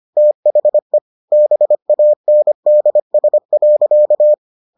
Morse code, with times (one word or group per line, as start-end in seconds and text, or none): 0.2-1.0 THE
1.3-4.3 BANDS.